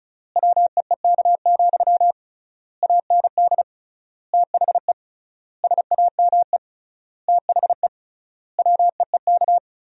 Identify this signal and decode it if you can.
Morse code, heard as a keyed tone, about 36 wpm, 700 Hz. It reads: WEEK, AND THE SAME THE WEEK
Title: 3x Word-Spacing - Sentences from Top 500 Words - 35wpm